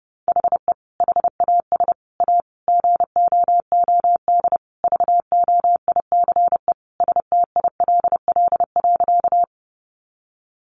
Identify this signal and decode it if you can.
Morse code, heard as a keyed tone, about 30 wpm, 700 Hz. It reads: HE HAS A GOOD VOICE STILL.
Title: Sentences from Top 500 Words 30wpm